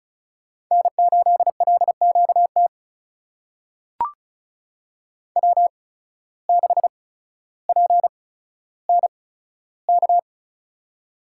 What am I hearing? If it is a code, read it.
Morse code, heard as a keyed tone, about 36 wpm, spaced as for 15 wpm, 700 Hz. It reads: N8LQT W6PNK
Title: USA Call Signs - 5 Characters - 8x Character Spacing - 35wpm